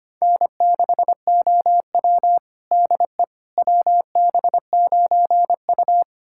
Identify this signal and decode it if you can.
Morse code, heard as a keyed tone, about 25 wpm, 700 Hz. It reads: N6OW DE WB9U
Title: USA Callsign DE Callsign - 4 Characters - 25wpm